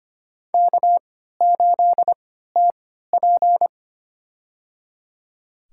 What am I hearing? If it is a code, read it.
Morse code, heard as a keyed tone, about 25 wpm, spaced as for 14 wpm, 700 Hz. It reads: K8TP
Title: USA Call Signs - 4 Characters - 3x Character Spacing - 25wpm